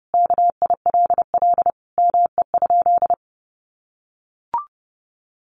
Morse code, heard as a keyed tone, about 30 wpm, 700 Hz. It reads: KILL ME?
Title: Sentences from Top 1400 Words 30wpm